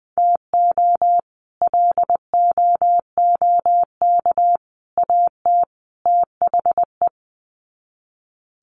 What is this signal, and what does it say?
Morse code, heard as a keyed tone, about 20 wpm, 700 Hz. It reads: TO LOOK AT THE
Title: Sets of 4 Words 20wpm